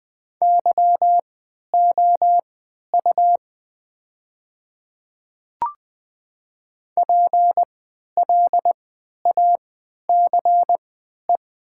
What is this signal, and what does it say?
Morse code, heard as a keyed tone, about 20 wpm, spaced as for 11 wpm, 700 Hz. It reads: YOU PLACE
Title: 3x Character Spacing - Top 500 Words - 20wpm